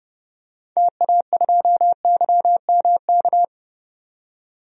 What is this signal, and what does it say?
Morse code, heard as a keyed tone, about 31 wpm, 700 Hz. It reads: TA2YMK
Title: DXCC-100 Call Signs - 6 Characters - 30wpm